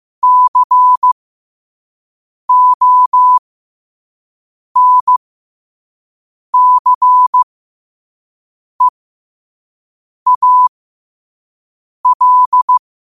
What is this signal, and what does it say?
Morse code, heard as a keyed tone, about 15 wpm, spaced as for 5 wpm, 1000 Hz. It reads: CONCEAL
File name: qotdfiles / 5wpm